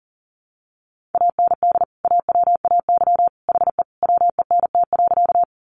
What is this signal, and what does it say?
Morse code, heard as a keyed tone, about 40 wpm, 700 Hz. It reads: AND AWAY HE WENT.
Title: Sentences from Top 600 Words 40wpm